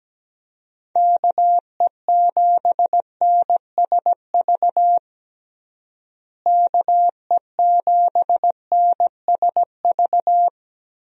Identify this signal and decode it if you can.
Morse code, heard as a keyed tone, about 17 wpm, 700 Hz. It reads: KE7NSV KE7NSV
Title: USA Call Signs - 6 Characters - Repeated 2x - 17wpm